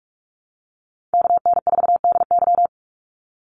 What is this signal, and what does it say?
Morse code, heard as a keyed tone, about 46 wpm, 700 Hz. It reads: KN4DY
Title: USA Call Signs 45wpm